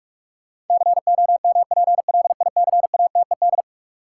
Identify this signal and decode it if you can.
Morse code, heard as a keyed tone, about 46 wpm, 700 Hz. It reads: COMPLICATED
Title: Sentences from Top 1200 Words - New Word Drill Alt - 45wpm